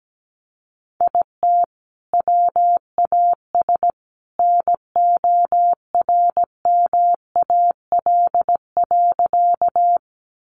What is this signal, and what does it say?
Morse code, heard as a keyed tone, about 17 wpm, 700 Hz. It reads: IT WAS NORMAL.
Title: Sentences from Top 1200 Words 17wpm